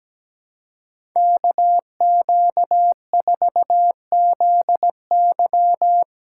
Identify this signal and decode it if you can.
Morse code, heard as a keyed tone, about 17 wpm, 700 Hz. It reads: KQ4ZY
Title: USA Callsign DE Callsign - 5 Characters - 17wpm